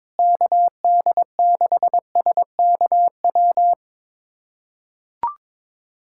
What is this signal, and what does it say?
Morse code, heard as a keyed tone, about 22 wpm, 700 Hz. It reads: KD6SKW
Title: USA Callsign DE Callsign - 6 Characters - 22wpm